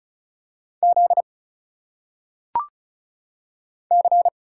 Morse code, heard as a keyed tone, about 34 wpm, 700 Hz. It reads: Z C
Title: Single Letters - Phonetic Pronunciation 35wpm